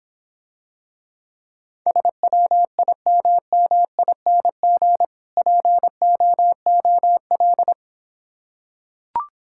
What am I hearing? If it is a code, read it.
Morse code, heard as a keyed tone, about 26 wpm, 700 Hz. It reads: SWIMMING POOL